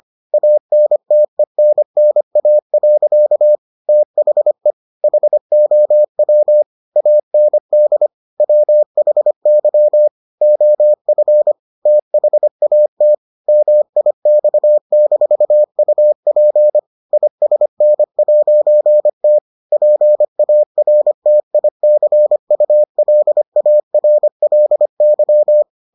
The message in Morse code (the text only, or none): ANTENNA. THE HOW AND WHY OF THAT MIX-UP ISN'T PARTICULARLY